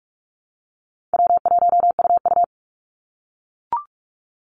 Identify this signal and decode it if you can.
Morse code, heard as a keyed tone, about 44 wpm, 700 Hz. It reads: W1UU
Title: Call Signs - 45wpm